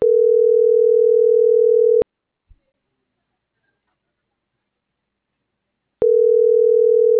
Ringback tone (North America), 440 + 480 Hz.